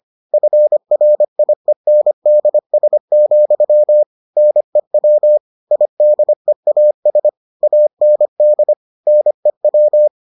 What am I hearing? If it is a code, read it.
Morse code, heard as a keyed tone, about 25 wpm, 600 Hz. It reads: FRIENDS, NEW IDEAS AND NEW